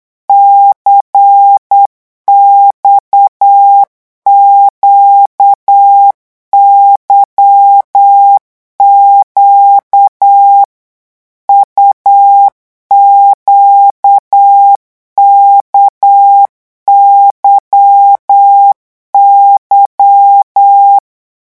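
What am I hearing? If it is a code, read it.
Morse code, heard as a keyed tone, about 8 wpm, 800 Hz. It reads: CXQYQ UQKYY
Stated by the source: Morse code revision lesson at 8 words per minute, from a French amateur radio course.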